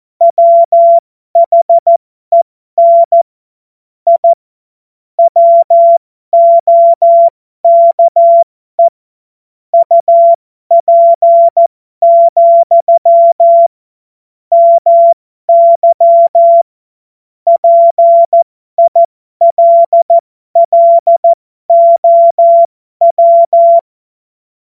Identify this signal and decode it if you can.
Morse code, heard as a keyed tone, about 14 wpm, spaced as for 12 wpm, 680 Hz. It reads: WHEN I WOKE UP, MY PILLOW
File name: Morsekurs / Witze